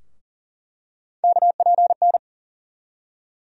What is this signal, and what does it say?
Morse code, heard as a keyed tone, about 40 wpm, 700 Hz. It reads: KPN
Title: USA Call Sign Postfixes - 3 Letters 40wpm